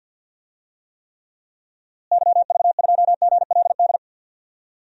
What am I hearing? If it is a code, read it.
Morse code, heard as a keyed tone, about 50 wpm, 700 Hz. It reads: YV2GLD
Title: DXCC-100 Call Signs - 6 Characters - Repeated 2x - 50wpm